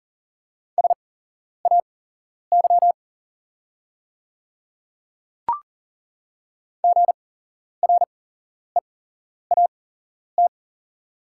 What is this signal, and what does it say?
Morse code, heard as a keyed tone, about 40 wpm, spaced as for 11 wpm, 700 Hz. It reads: SAY GREAT
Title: 8x Character Spacing - Top 300 Words - 40wpm